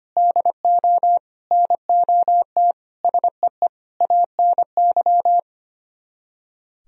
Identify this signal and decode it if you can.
Morse code, heard as a keyed tone, about 33 wpm, 700 Hz. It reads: DO NOT SEE ANY